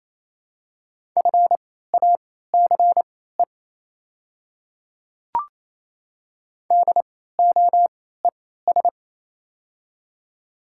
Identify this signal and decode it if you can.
Morse code, heard as a keyed tone, about 28 wpm, spaced as for 16 wpm, 700 Hz. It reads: FACE DOES
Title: 3x Character Spacing - Top 300 Words - 28wpm